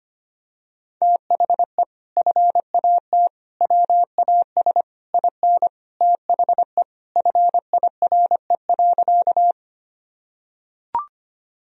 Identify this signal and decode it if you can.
Morse code, heard as a keyed tone, about 25 wpm, 700 Hz. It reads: THE FAT WAS IN THE FIRE.
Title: Sentences from Top 1000 Words - Review Part 4 - 25wpm